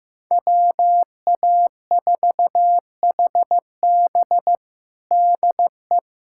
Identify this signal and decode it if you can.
Morse code, heard as a keyed tone, about 15 wpm, 700 Hz. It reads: WA4HB DE
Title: USA Callsign DE Callsign - 5 Characters - 15wpm